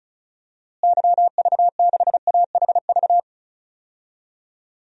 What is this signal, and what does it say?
Morse code, heard as a keyed tone, about 36 wpm, 700 Hz. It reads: YV6AHV
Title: DXCC-100 Call Signs - 6 Characters - Repeated 2x - 35wpm